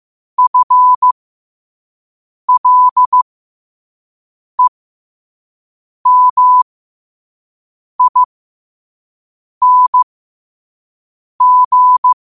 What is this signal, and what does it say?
Morse code, heard as a keyed tone, about 15 wpm, spaced as for 5 wpm, 1000 Hz. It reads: FLEMING
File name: qotdfiles / 5wpm